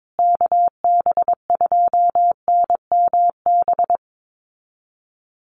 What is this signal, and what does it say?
Morse code, heard as a keyed tone, about 22 wpm, 700 Hz. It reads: KB2NMB
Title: USA Call Signs Repeated 2x 22wpm